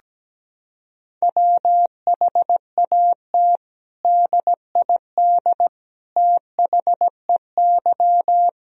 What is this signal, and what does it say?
Morse code, heard as a keyed tone, about 17 wpm, 700 Hz. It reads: WHAT DID THEY
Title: Sentences from Top 1000 Words - Review Part 4 - 17wpm